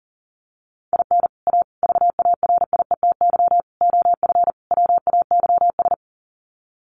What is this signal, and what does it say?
Morse code, heard as a keyed tone, about 40 wpm, 700 Hz. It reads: IN A VARIETY OF WAYS
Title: Sets of 5 Words Encore 40wpm